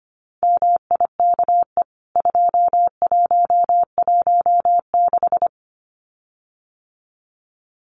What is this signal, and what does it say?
Morse code, heard as a keyed tone, about 25 wpm, 700 Hz. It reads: MIKE 2116
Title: CWT Exchange 25wpm